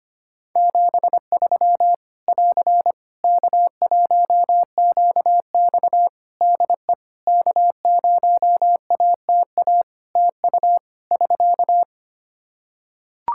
Morse code, heard as a keyed tone, about 25 wpm, 700 Hz. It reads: 73 <AR> K1QX DE K0ATA TU <SK>
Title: Practice QSO - Exchange 4 - Ending - 25wpm